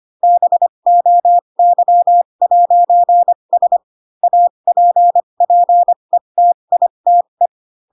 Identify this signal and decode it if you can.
Morse code, heard as a keyed tone, about 25 wpm, spaced as for 22 wpm, 700 Hz. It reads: BOY'S APPETITE